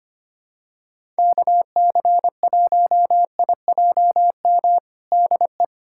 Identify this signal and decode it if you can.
Morse code, heard as a keyed tone, about 25 wpm, 700 Hz. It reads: KC1IJM DE
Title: USA Callsign DE Callsign - 6 Characters - 25wpm